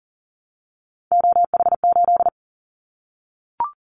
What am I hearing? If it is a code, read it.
Morse code, heard as a keyed tone, about 40 wpm, 700 Hz. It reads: OH8
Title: Callsign Prefixes for Top 25 DXCC - 40wpm